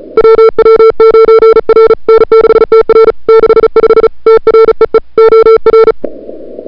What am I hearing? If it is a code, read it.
Morse code, heard as a keyed tone, about 35 wpm, 440 Hz. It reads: WW9R N6TR 65 TREE OR
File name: CWSprint / February 2007 CW Sprint / 01-2007 CW Sprint Hour1